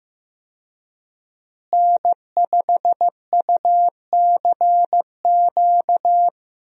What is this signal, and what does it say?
Morse code, heard as a keyed tone, about 15 wpm, 700 Hz. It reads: N5UCQ
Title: USA Call Signs - 5 Characters - Repeated 3x - 15wpm